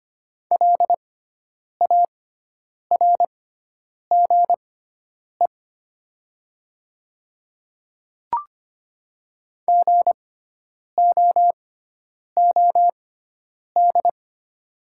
Morse code, heard as a keyed tone, about 25 wpm, spaced as for 9 wpm, 700 Hz. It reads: LARGE GOOD
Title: 6x Character Spacing - Top 200 Words - 25wpm